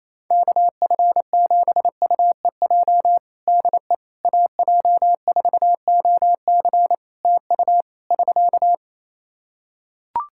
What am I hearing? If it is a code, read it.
Morse code, heard as a keyed tone, about 28 wpm, 700 Hz. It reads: KF7UEJ DE AJ4OC TU <SK>
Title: Practice QSO - Exchange 4 - Ending - 28wpm